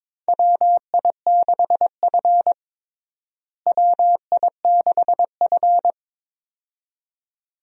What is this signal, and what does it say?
Morse code, heard as a keyed tone, about 22 wpm, 700 Hz. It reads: WI6F WI6F
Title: USA Call Signs - 4 Characters - Repeated 2x - 22wpm